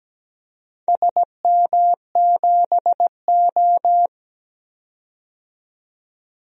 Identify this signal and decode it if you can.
Morse code, heard as a keyed tone, about 17 wpm, 700 Hz. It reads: SM7O
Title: DXCC-100 Call Signs - 4 Characters - 17wpm